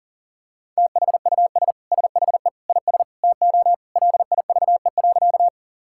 Morse code, heard as a keyed tone, about 40 wpm, 700 Hz. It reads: THUS SHE IS TO LIVE.
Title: Sentences from Top 1000 Words - Review Part 5 - 40wpm